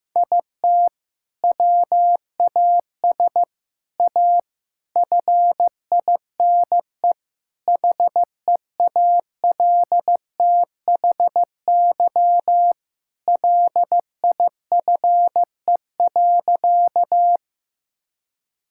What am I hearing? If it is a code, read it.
Morse code, heard as a keyed tone, about 15 wpm, 700 Hz. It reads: IT WAS A FINE HEALTHY LIFE.
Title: Sentences from Top 1000 Words - Review Part 5 - 15wpm